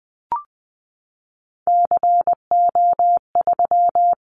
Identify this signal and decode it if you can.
Morse code, heard as a keyed tone, about 19 wpm, 700 Hz. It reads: CO3
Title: Callsign Prefixes for Top 100 DXCC - 20wpm